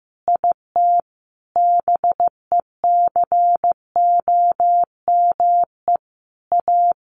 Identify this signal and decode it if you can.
Morse code, heard as a keyed tone, about 15 wpm, 700 Hz. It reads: IT BECOME A
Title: Sentences from Top 1200 Words 15wpm